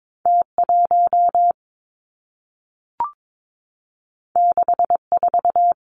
Morse code, heard as a keyed tone, about 22 wpm, 700 Hz. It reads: T 1 6 4
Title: Two Challenging Letter-Number Combinations 22wpm